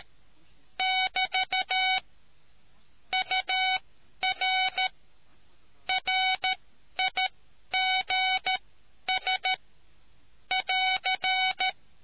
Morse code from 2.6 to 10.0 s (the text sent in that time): UR RIGS